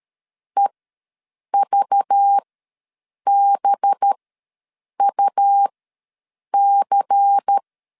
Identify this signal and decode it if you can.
Morse code, heard as a keyed tone, about 13 wpm, spaced as for 7 wpm, 800 Hz. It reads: EVBUC
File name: LEZIONI / 5 LBVC / 5 LBVC ws 12wpm 3159